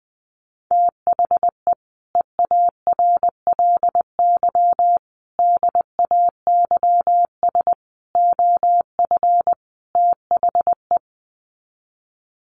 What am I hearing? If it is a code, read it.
Morse code, heard as a keyed tone, about 20 wpm, 700 Hz. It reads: THE EARLY DAYS OF THE
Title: Sets of 5 Words Encore 20wpm